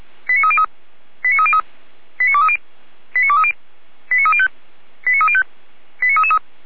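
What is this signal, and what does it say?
Five-tone selective call (ZVEI), heard as quick runs of five tones, tones of about 70 ms. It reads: 89303 89303 89230 89230 89305 89305
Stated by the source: ZVEI five-tone selective call received by SDR, from the Feuerwehr Orlamünde (feuerwehr-orlamuende) audio server.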